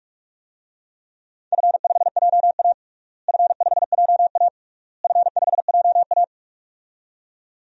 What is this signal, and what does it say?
Morse code, heard as a keyed tone, about 43 wpm, 700 Hz. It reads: F5JA F5JA F5JA
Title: DXCC-100 Call Signs - 4 Characters - Repeated 3x - 45wpm